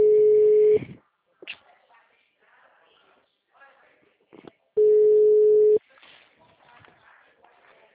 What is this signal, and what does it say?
Ringback tone (Europe, 425 Hz), with speech.